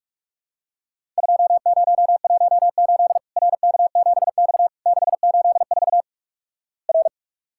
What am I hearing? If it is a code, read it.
Morse code, heard as a keyed tone, about 46 wpm, 700 Hz. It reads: 2018 RK7X 684 R
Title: Worked All Europe DX Contest - QTC Exchange - 45wpm